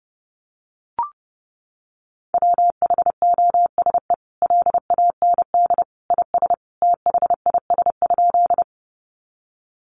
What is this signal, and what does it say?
Morse code, heard as a keyed tone, about 28 wpm, 700 Hz. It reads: WHOSE LAND IS THIS?
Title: Sentences from Top 1200 Words 30wpm